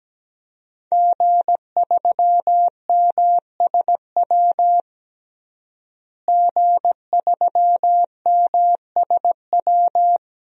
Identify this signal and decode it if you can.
Morse code, heard as a keyed tone, about 17 wpm, 700 Hz. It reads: G3MSW G3MSW
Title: DXCC-100 Call Signs - 5 Characters - Repeated 3x - 17wpm